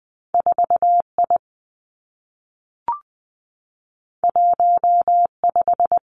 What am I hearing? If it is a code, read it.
Morse code, heard as a keyed tone, about 19 wpm, 700 Hz. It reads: 4 I 1 5
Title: Two Challenging Letter-Number Combinations 20wpm